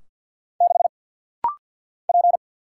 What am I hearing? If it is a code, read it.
Morse code, heard as a keyed tone, about 50 wpm, 690 Hz. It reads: 6 P
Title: Single Letter-Number Alt - Rapid-Fire 50wpm